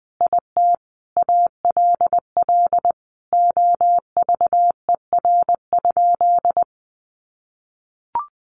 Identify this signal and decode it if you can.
Morse code, heard as a keyed tone, about 20 wpm, 700 Hz. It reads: IT ALL OVER?
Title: Sentences from Top 1200 Words 20wpm